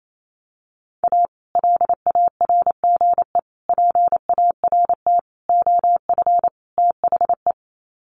Morse code, heard as a keyed tone, about 28 wpm, 700 Hz. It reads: A LARGE PART OF THE
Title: Sets of 5 Words Encore2 28wpm